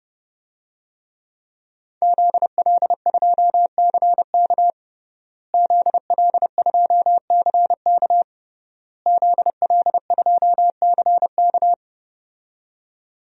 Morse code, heard as a keyed tone, about 32 wpm, 700 Hz. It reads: ZL2CK ZL2CK ZL2CK
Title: DXCC-100 Call Signs - 5 Characters - Repeated 3x - 30wpm